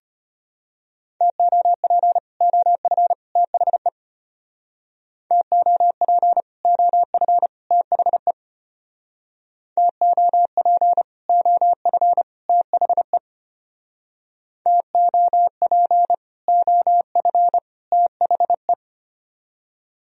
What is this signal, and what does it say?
Morse code, heard as a keyed tone, about 32 wpm, 700 Hz. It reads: TOP OF THE TOP OF THE TOP OF THE TOP OF THE